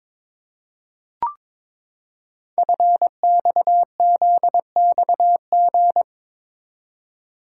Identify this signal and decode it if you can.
Morse code, heard as a keyed tone, about 22 wpm, 700 Hz. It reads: FXZXG
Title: Code Groups - 5 Letters - 22wpm